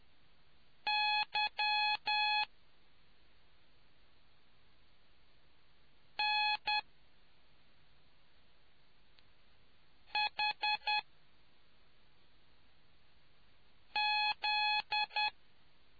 Morse code, heard as a keyed tone, about 10 wpm, spaced as for 2 wpm, 840 Hz. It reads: YNHZ